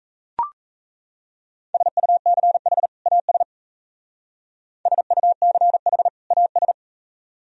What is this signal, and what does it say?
Morse code, heard as a keyed tone, about 36 wpm, 700 Hz. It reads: SUCH AS SUCH AS